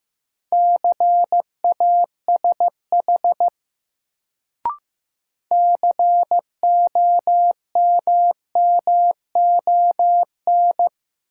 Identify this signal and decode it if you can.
Morse code, heard as a keyed tone, about 15 wpm, 700 Hz. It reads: CASH COMMON
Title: Sentences from Top 700 Words - New Word Drill - 15wpm